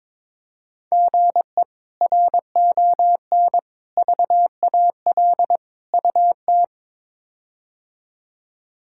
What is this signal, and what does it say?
Morse code, heard as a keyed tone, about 21 wpm, 700 Hz. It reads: GE RON VAL UT
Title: K1USN SST QSO Practice - 22wpm